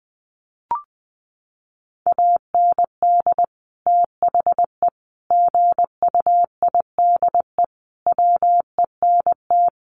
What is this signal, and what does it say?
Morse code, heard as a keyed tone, about 20 wpm, 700 Hz. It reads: AND THE GUIDE WENT
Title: Sentences from Top 600 Words 20wpm